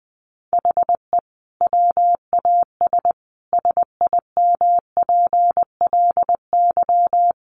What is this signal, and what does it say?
Morse code, heard as a keyed tone, about 20 wpm, 700 Hz. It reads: HE WAS SIMPLY
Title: Sentences from Top 1500 Words 20wpm